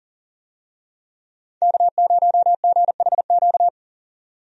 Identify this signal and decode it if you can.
Morse code, heard as a keyed tone, about 40 wpm, 700 Hz. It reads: K0GHQ
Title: USA Call Signs - 5 Characters - 40wpm